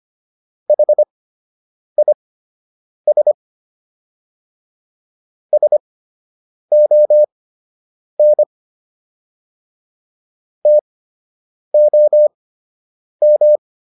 Morse code, heard as a keyed tone, about 25 wpm, spaced as for 8 wpm, 600 Hz. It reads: HIS SON TOM